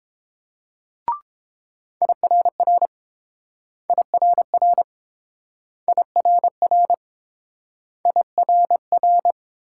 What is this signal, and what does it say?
Morse code, heard as a keyed tone, about 30 wpm, 700 Hz. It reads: IRR IRR IRR IRR